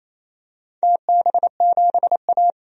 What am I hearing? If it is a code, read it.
Morse code, heard as a keyed tone, about 27 wpm, 690 Hz. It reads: TB7A
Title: DXCC-100 Call Signs - 4 Characters - 28wpm